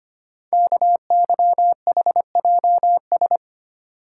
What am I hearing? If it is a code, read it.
Morse code, heard as a keyed tone, about 25 wpm, 690 Hz. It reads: KYHJS
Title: Code Groups - 5 Letters - 25wpm